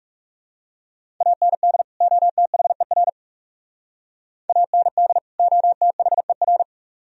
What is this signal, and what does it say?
Morse code, heard as a keyed tone, about 40 wpm, 700 Hz. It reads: AND OTHER AND OTHER